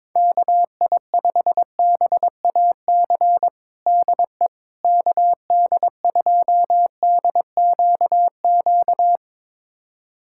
Morse code, heard as a keyed tone, about 22 wpm, 700 Hz. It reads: KI5BAC DE KD2DQQ
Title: USA Callsign DE Callsign - 6 Characters - 22wpm